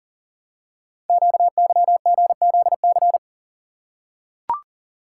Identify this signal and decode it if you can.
Morse code, heard as a keyed tone, about 42 wpm, 690 Hz. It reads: QYGZC
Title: Code Groups - 5 Letters - 40wpm